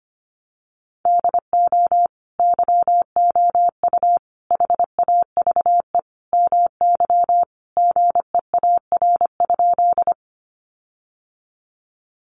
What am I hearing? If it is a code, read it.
Morse code, heard as a keyed tone, about 25 wpm, 700 Hz. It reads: DO YOU HAVE MY GEAR?
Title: Sentences from Top 1400 Words 25wpm